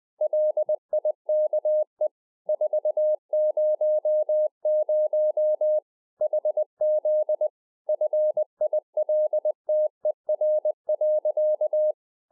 Morse code, heard as a keyed tone, about 20 wpm, 610 Hz. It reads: LIKE 400 HZ FILTER.
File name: 2011-04-mp3 / A SMALL TRANCEIVER 3 20WPM